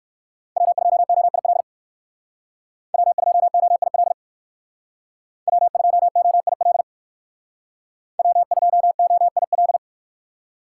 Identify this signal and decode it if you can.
Morse code, heard as a keyed tone, about 57 wpm, 700 Hz. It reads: W2OIL W2OIL W2OIL W2OIL